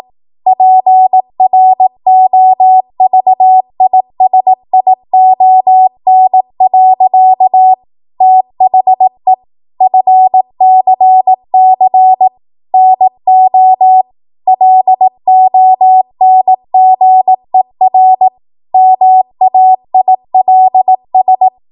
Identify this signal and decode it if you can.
Morse code, heard as a keyed tone, about 18 wpm, 750 Hz. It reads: PROVISION. THE FCC NO LONGER MAILS
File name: Morse / 18WPM